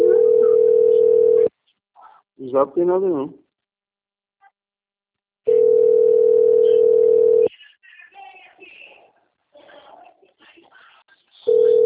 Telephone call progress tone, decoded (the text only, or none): ringback tone